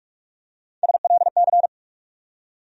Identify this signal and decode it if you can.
Morse code, heard as a keyed tone, about 48 wpm, 700 Hz. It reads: SLC
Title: USA Call Sign Postfixes - 3 Letters 45wpm